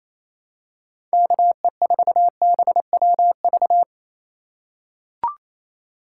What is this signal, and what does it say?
Morse code, heard as a keyed tone, about 28 wpm, 700 Hz. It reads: KE4BWV
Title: USA Call Signs - 6 Characters - 2x Character Spacing - 28wpm